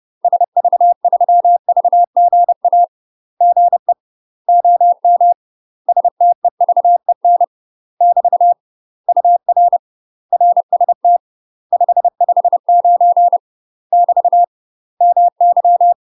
Morse code, heard as a keyed tone, about 30 wpm, 700 Hz. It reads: SV3VGA GE OM STEVEN <BT> UR RST 559 <BT> MY
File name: UK MASTER QSOs TRAINING / 30WPM / UK MASTER QSOs 0015